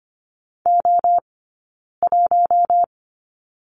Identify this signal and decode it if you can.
Morse code, heard as a keyed tone, about 25 wpm, spaced as for 9 wpm, 690 Hz. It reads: O1
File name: course / 25:10wpm / 092 Lesson 16d - Letter c - Two Characters 25wpm